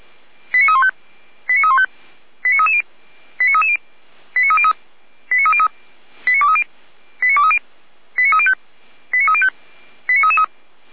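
Five-tone selective call (ZVEI), heard as quick runs of five tones, tones of about 70 ms. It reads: 89315 89315 89330 89330 89303 89303 89230 89230 89305 89305 89333
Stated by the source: ZVEI five-tone selective call received with an SDR.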